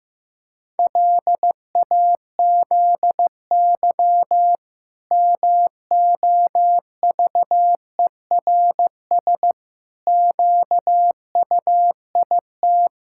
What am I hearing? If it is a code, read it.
Morse code, heard as a keyed tone, about 15 wpm, 700 Hz. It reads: LAZY MOVERS QUIT
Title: Pangram Challenge v2 - 15wpm